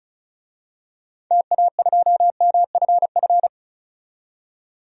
Morse code, heard as a keyed tone, about 36 wpm, 700 Hz. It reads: TA2MFF
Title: DXCC-100 Call Signs - 6 Characters - 35wpm